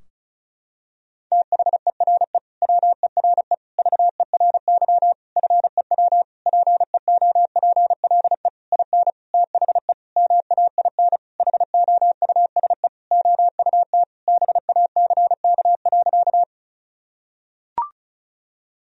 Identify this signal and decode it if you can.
Morse code, heard as a keyed tone, about 34 wpm, 700 Hz. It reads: THERE WERE VERY FEW PEOPLE IN THE MAIN HOUSE OUT BACK.
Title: Sentences from Top 1000 Words - Review Part 2 - 35wpm